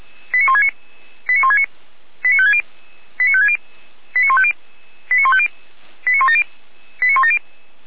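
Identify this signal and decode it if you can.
Five-tone selective call (ZVEI), heard as quick runs of five tones, tones of about 70 ms. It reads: 89169 89169 89560 89560 89150 89150 89170 89170
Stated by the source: ZVEI five-tone selective call received on SDR, archived by Feuerwehr Orlamünde (F85360).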